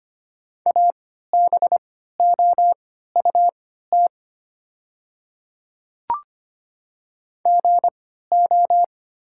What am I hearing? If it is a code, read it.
Morse code, heard as a keyed tone, about 25 wpm, spaced as for 14 wpm, 700 Hz. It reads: ABOUT GO
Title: 3x Character Spacing - Top 100 Words - 25wpm